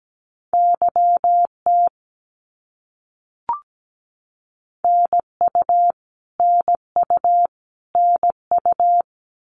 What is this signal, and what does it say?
Morse code, heard as a keyed tone, about 17 wpm, 700 Hz. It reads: YT NU NU NU
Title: Canadian Provinces and Territories - Repeated 3x - 17wpm